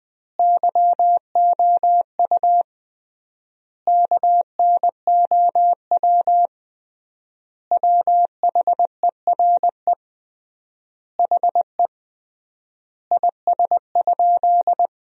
Morse code, heard as a keyed tone, about 20 wpm, 700 Hz. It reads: YOU KNOW WHERE HE IS?
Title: 3x Word-Spacing - Sentences from Top 300 Words - 20wpm